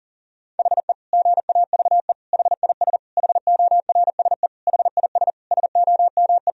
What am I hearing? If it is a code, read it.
Morse code, heard as a keyed tone, about 37 wpm, 700 Hz. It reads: HE GAVE HIS HORSE HIS SOME
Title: Sentences from Top 1000 Words - Review Part 1 - 40wpm